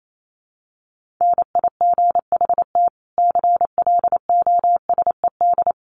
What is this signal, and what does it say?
Morse code, heard as a keyed tone, about 28 wpm, 700 Hz. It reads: NIGHT CLOSED
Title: Sentences from Top 1400 Words 28wpm